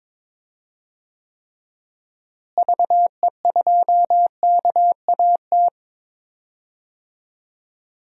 Morse code, heard as a keyed tone, about 22 wpm, 700 Hz. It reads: VE2KAT
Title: DXCC-100 Call Signs - 6 Characters - Repeated 2x - 22wpm